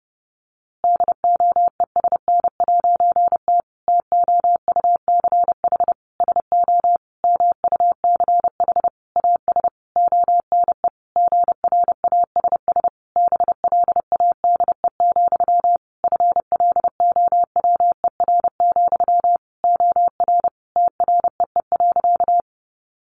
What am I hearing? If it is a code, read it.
Morse code, heard as a keyed tone, about 30 wpm, 700 Hz. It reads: DOESN'T TOUCH SO MUCH AS ONE GRASS BLADE, FLOWER, OR TREE.
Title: A Sound Of Thunder 30wpm